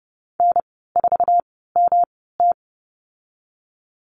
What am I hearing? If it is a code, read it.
Morse code, heard as a keyed tone, about 30 wpm, spaced as for 17 wpm, 700 Hz. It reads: N4MT
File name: Lesson 19g - Letter w - Call Signs 30wpm